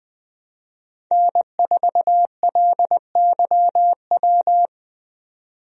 Morse code, heard as a keyed tone, about 20 wpm, 700 Hz. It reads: N4LYW
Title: USA Call Signs - 5 Characters - 20wpm